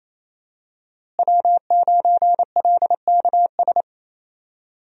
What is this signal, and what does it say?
Morse code, heard as a keyed tone, about 28 wpm, 700 Hz. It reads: W9LKS
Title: USA Call Signs - 5 Characters - Repeated 2x - 28wpm